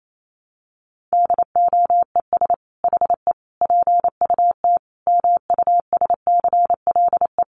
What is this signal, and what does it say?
Morse code, heard as a keyed tone, about 28 wpm, 700 Hz. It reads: DOES HE PUT MUSCLE
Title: Sentences from Top 1300 Words 28wpm